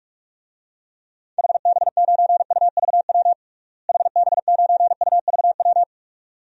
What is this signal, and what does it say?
Morse code, heard as a keyed tone, about 46 wpm, 700 Hz. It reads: HB9UVW HB9UVW
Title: DXCC-100 Call Signs - 6 Characters - Repeated 2x - 45wpm